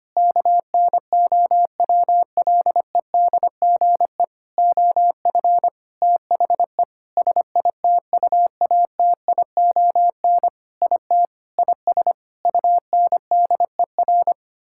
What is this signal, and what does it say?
Morse code, heard as a keyed tone, about 25 wpm, 700 Hz. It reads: KNOWLEDGE OF THE SITUATION IT IS UNDER
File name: practice / 25wpm / Sentences with 9-Word Phrases - 25wpm